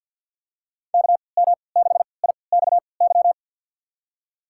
Morse code, heard as a keyed tone, about 51 wpm, spaced as for 28 wpm, 700 Hz. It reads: KM6IXY
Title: USA Call Signs - 3x Character Spacing - 50wpm